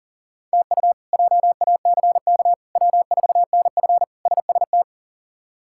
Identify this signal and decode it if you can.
Morse code, heard as a keyed tone, about 40 wpm, 700 Hz. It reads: TU JACK W4NF SST
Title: K1USN SST QSO Practice - 40wpm